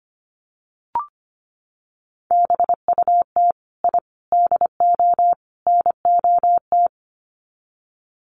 Morse code, heard as a keyed tone, about 25 wpm, 700 Hz. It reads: BUT I DO NOT
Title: Sets of 4 Words 25wpm